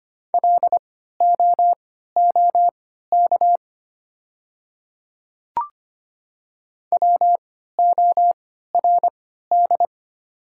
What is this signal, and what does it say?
Morse code, heard as a keyed tone, about 25 wpm, spaced as for 14 wpm, 700 Hz. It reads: LOOK WORD
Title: 3x Character Spacing - Top 100 Words - 25wpm